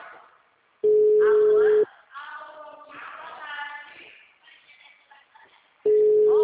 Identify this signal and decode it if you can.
Ringback tone (Europe, 425 Hz), with speech.
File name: out- -2041- - - .47622